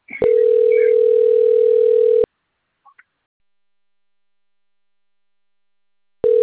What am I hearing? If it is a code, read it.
Ringback tone (North America), 440 + 480 Hz.